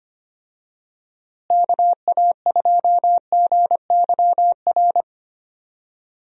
Morse code, heard as a keyed tone, about 25 wpm, 700 Hz. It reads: KA2GYR
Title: USA Call Signs - 6 Characters - Repeated 3x - 25wpm